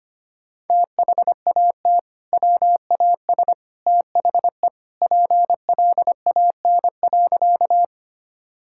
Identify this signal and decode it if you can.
Morse code, heard as a keyed tone, about 25 wpm, 700 Hz. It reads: THAT WAS THE PLAN.
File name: practice / 25wpm / Sentences with 2-Word Phrases - 25wpm